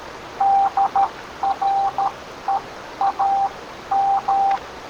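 Morse code, heard as a keyed tone, about 13 wpm, spaced as for 11 wpm, 790 Hz. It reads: DREAM
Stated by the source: Morse from claimed EVP spirit communication.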